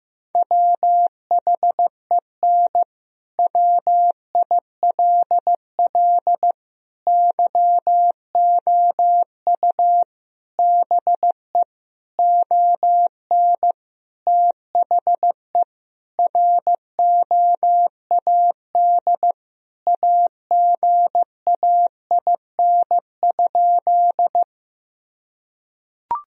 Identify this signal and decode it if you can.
Morse code, heard as a keyed tone, about 16 wpm, 700 Hz. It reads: WHEN WILL YOU BE ON THE ROAD AGAIN?
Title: Sentences from Top 1000 Words - Review Part 2 - 15wpm